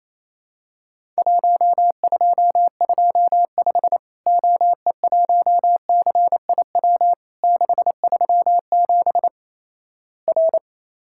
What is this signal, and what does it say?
Morse code, heard as a keyed tone, about 27 wpm, 700 Hz. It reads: 1225 OE1CIW 637 R
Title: Worked All Europe DX Contest - QTC Exchange - 28wpm